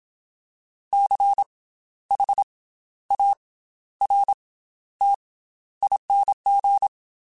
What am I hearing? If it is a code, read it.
Morse code, heard as a keyed tone, about 27 wpm, 800 Hz. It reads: C H A R T ING
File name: Special INGs / Special INGs-mp3s / Slow ing @25